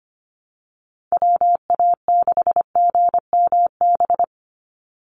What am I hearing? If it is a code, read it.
Morse code, heard as a keyed tone, about 25 wpm, 700 Hz. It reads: WA6GMB
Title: USA Call Signs 25wpm